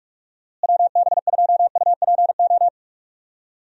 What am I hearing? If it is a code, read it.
Morse code, heard as a keyed tone, about 43 wpm, 700 Hz. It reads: WB2UPO
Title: USA Call Signs - 6 Characters - Repeated 3x - 45wpm